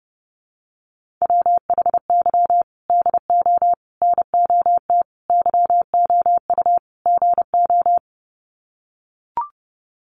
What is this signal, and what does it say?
Morse code, heard as a keyed tone, about 30 wpm, 700 Hz. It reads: WHY DO NOT YOU GO
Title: Sets of 5 Words Encore 30wpm